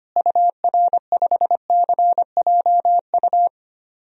Morse code, heard as a keyed tone, about 25 wpm, 700 Hz. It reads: UR5CJU
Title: DXCC-100 Call Signs - 6 Characters - Repeated 2x - 25wpm